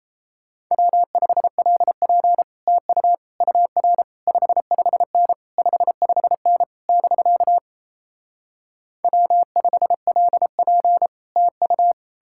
Morse code, heard as a keyed tone, about 37 wpm, 700 Hz. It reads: W5LP TU UR 55N 55N <BK> W5LP TU